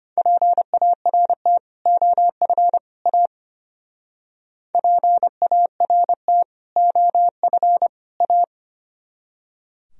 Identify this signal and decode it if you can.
Morse code, heard as a keyed tone, about 31 wpm, 700 Hz. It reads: PART OF A PART OF A
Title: Speed-Racing - Sets of 3 Words 25wpm